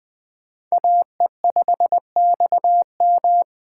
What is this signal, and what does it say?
Morse code, heard as a keyed tone, about 21 wpm, 700 Hz. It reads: AE5XM
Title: USA Call Signs - 5 Characters - 4x Character Spacing - 20wpm